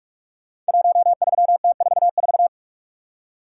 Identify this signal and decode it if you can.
Morse code, heard as a keyed tone, about 44 wpm, 700 Hz. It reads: 13T44
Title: Code Groups - 5 Letter:Numbers - 45wpm